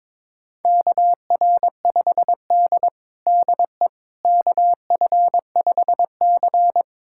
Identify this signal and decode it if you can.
Morse code, heard as a keyed tone, about 22 wpm, 700 Hz. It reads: KR5D DE KF5C
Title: USA Callsign DE Callsign - 4 Characters - 22wpm